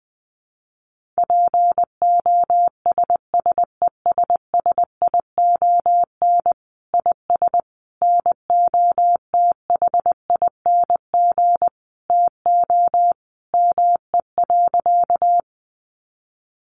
Morse code, heard as a keyed tone, about 20 wpm, 700 Hz. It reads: POSSESSION IS NOTHING TO ME.
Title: Sentences from Top 1400 Words 20wpm